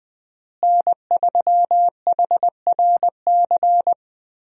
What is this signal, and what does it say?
Morse code, heard as a keyed tone, about 20 wpm, 700 Hz. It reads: N3HRC
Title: USA Call Signs - 5 Characters - Repeated 3x - 20wpm